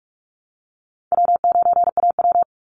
Morse code, heard as a keyed tone, about 44 wpm, 700 Hz. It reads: W9AW
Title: Call Signs Repeated 3x - 45wpm